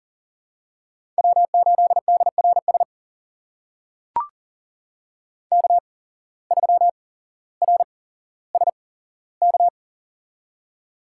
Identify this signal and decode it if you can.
Morse code, heard as a keyed tone, about 40 wpm, spaced as for 17 wpm, 700 Hz. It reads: W8DRS K3RSK
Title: USA Call Signs - 5 Characters - 8x Character Spacing - 40wpm